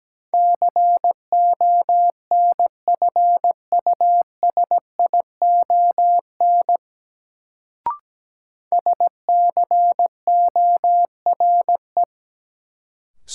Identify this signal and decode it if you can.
Morse code, heard as a keyed tone, about 17 wpm, 700 Hz. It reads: CONFUSION SCORE
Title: Sentences from Top 1600 Words - New Word Drill - 17wpm